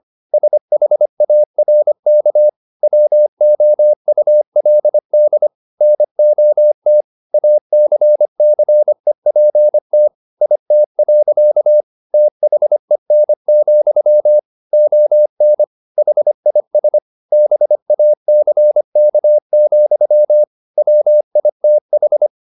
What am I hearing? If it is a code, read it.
Morse code, heard as a keyed tone, about 25 wpm, 600 Hz. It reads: SHARK WOULD NOT ACCEPT IT. THEN, ON HIS BACK, WITH